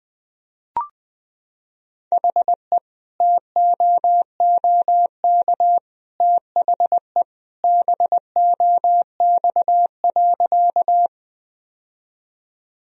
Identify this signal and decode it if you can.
Morse code, heard as a keyed tone, about 21 wpm, 700 Hz. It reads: HE TOOK THE BOX.
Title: Sentences from Top 1000 Words - Review Part 2 - 20wpm